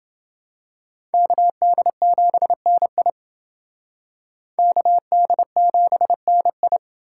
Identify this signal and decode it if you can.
Morse code, heard as a keyed tone, about 28 wpm, 700 Hz. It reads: KD7NI KD7NI